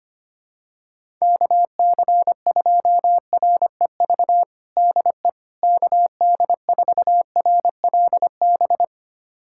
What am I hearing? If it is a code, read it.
Morse code, heard as a keyed tone, about 25 wpm, 700 Hz. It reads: KC2REV DE KD4RLB
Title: USA Callsign DE Callsign - 6 Characters - 25wpm